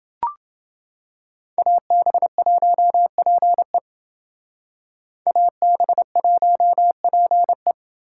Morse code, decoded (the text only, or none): AB1PE AB1PE